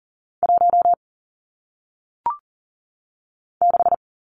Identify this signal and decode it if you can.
Morse code, heard as a keyed tone, about 40 wpm, 700 Hz. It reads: 1 6
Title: Numbers - 1-Digit 40wpm